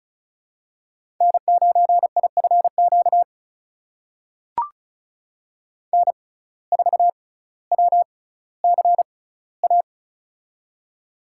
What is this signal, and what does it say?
Morse code, heard as a keyed tone, about 36 wpm, spaced as for 18 wpm, 700 Hz. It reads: N9IFQ N4WCA
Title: USA Call Signs - 5 Characters - 6x Character Spacing - 35wpm